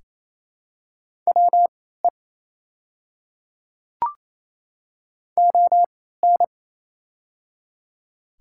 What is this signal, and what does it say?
Morse code, heard as a keyed tone, about 28 wpm, spaced as for 16 wpm, 700 Hz. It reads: WE ON
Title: 3x Character Spacing - Top 100 Words - 28wpm